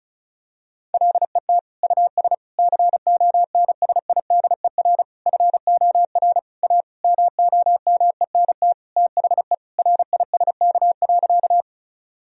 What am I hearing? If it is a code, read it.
Morse code, heard as a keyed tone, about 36 wpm, 700 Hz. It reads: LET US CONSIDER FOR A MOMENT THE RISK.
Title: Sentences from Top 1000 Words - Review Part 4 - 35wpm